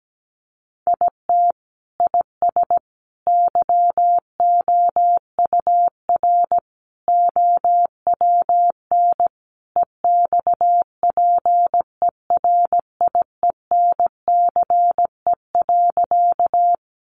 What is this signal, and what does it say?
Morse code, heard as a keyed tone, about 17 wpm, 700 Hz. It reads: IT IS YOUR OWN EXPERIENCE.
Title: Sentences from Top 600 Words 17wpm